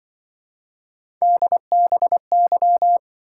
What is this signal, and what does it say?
Morse code, heard as a keyed tone, about 25 wpm, 700 Hz. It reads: DBY